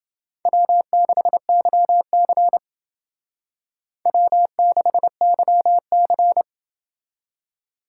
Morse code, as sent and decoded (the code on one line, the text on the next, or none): .-- -.... -.-- -.-. / .-- -.... -.-- -.-.
W6YC W6YC